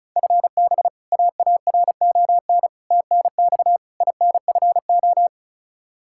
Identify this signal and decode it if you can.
Morse code, heard as a keyed tone, about 36 wpm, 700 Hz. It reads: FB AARON TNX INFO
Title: Practice QSO - Exchange 3 - More Info - 35wpm